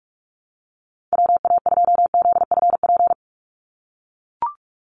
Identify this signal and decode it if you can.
Morse code, heard as a keyed tone, about 44 wpm, 700 Hz. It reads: WA2ZFP
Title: USA Call Signs 45wpm